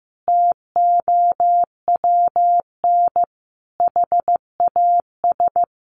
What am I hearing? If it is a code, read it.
Morse code, heard as a keyed tone, about 15 wpm, 700 Hz. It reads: TOWN HAS
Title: Sentences from Top 1500 Words 15wpm